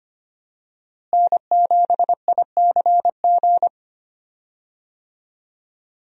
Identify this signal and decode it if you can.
Morse code, heard as a keyed tone, about 25 wpm, 700 Hz. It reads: N7ICG
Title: USA Call Signs - 5 Characters - Repeated 2x - 25wpm